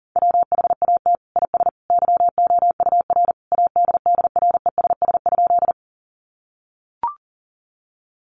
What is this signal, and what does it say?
Morse code, heard as a keyed tone, about 40 wpm, 700 Hz. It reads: WHAT IS YOUR ADDRESS?
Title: Sentences from Top 1000 Words 40wpm